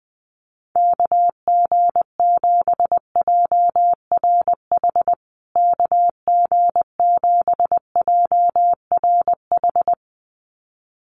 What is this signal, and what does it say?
Morse code, heard as a keyed tone, about 20 wpm, 700 Hz. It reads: KG7JRH KG7JRH
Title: USA Call Signs Repeated 2x 20wpm